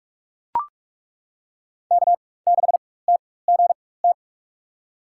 Morse code, heard as a keyed tone, about 42 wpm, spaced as for 20 wpm, 700 Hz. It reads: K6TGT
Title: USA Call Signs - 5 Characters - 4x Character Spacing - 45wpm